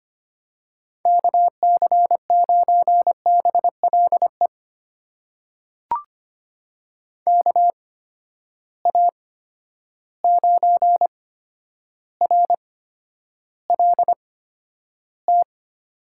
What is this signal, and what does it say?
Morse code, heard as a keyed tone, about 25 wpm, spaced as for 11 wpm, 700 Hz. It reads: KC9BLE KA9RLT
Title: USA Call Signs - 8x Character Spacing - 25wpm